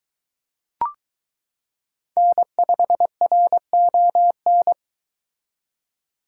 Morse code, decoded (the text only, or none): N5RON